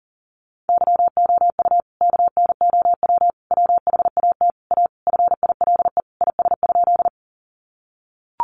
Morse code, heard as a keyed tone, about 40 wpm, 700 Hz. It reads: YOU KNOW WHAT A FILE IS?
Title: Sentences from Top 1200 Words 40wpm